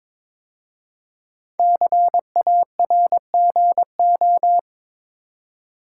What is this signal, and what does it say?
Morse code, heard as a keyed tone, about 22 wpm, 700 Hz. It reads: CARGO
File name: 5 Letter Words - First 3 Letters Given - 22wpm